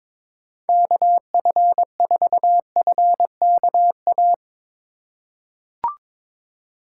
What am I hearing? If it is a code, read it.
Morse code, heard as a keyed tone, about 23 wpm, 700 Hz. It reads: KF4FKA
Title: USA Call Signs - 6 Characters - 3x Character Spacing - 22wpm